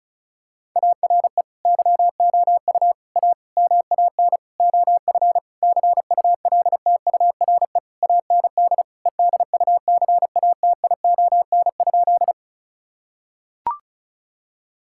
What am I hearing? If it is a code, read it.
Morse code, heard as a keyed tone, about 36 wpm, 700 Hz. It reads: ARE YOU A MAN OF CULTURE AND EDUCATION?
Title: Sentences from Top 1000 Words - Review Part 6 - 35wpm